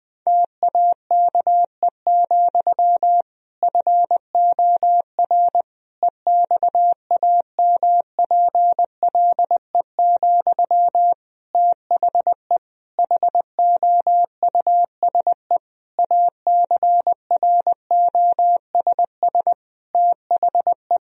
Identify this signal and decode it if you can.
Morse code, heard as a keyed tone, about 20 wpm, 700 Hz. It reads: TAKE, FOR EXAMPLE, THE HOUSE ACROSS THE
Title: Sentences from Top 1000 Words - Review Part 1 - 20wpm